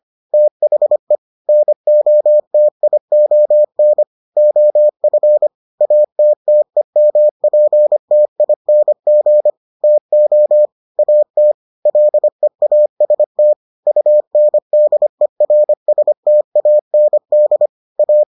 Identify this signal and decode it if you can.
Morse code, heard as a keyed tone, about 25 wpm, 600 Hz. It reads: THE NOTION OF ATTEMPTING TO AT LEAST UNDERSTAND A